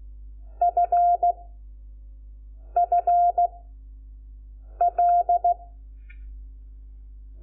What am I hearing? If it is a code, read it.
Morse code, heard as a keyed tone, about 16 wpm, 700 Hz. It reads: F F L
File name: Lecciones / CLASE Nº5